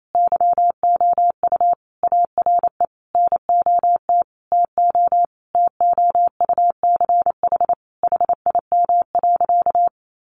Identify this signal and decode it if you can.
Morse code, heard as a keyed tone, about 28 wpm, 700 Hz. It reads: YOU ARE NOT TO TOUCH HIM.
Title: Sentences from Top 1200 Words 28wpm